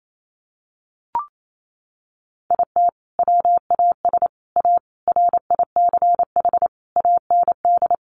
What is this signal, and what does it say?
Morse code, heard as a keyed tone, about 28 wpm, 700 Hz. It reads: IT WAS A RICH AND
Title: Sentences from Top 1100 Words 28wpm